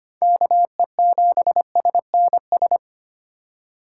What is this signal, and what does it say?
Morse code, heard as a keyed tone, about 25 wpm, 700 Hz. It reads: KE7SNS
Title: USA Call Signs - 6 Characters - 2x Character Spacing - 25wpm